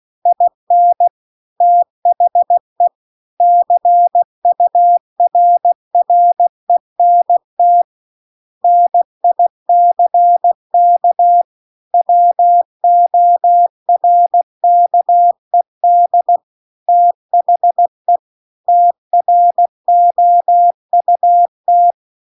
Morse code, heard as a keyed tone, about 16 wpm, 700 Hz. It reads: IN THE CURRENT NICK WORKED THE TROUT